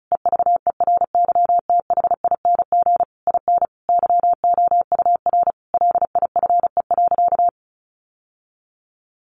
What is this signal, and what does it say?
Morse code, heard as a keyed tone, about 34 wpm, 700 Hz. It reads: EVERYTHING IN YOUR LIFE.
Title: Sentences from Top 700 Words 35wpm